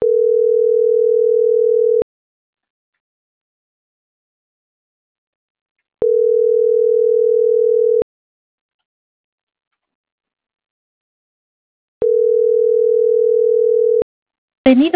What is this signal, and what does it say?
Ringback tone (North America), 440 + 480 Hz.